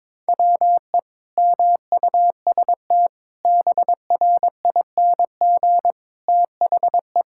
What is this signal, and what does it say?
Morse code, heard as a keyed tone, about 22 wpm, 700 Hz. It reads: WE MUST BRING THE
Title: Sentences from Top 1000 Words - Review Part 6 - 22wpm